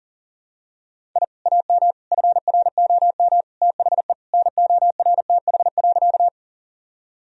Morse code, heard as a keyed tone, about 40 wpm, 700 Hz. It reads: I AM FROM THE NORTH.
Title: Sentences from Top 1000 Words - Review Part 1 - 40wpm